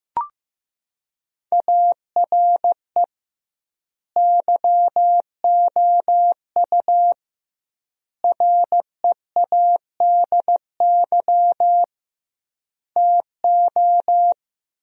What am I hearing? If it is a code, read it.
Morse code, heard as a keyed tone, about 16 wpm, 700 Hz. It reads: ARE YOU READY TO
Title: 2x Word-Spacing - Sentences from Top 400 Words - 15wpm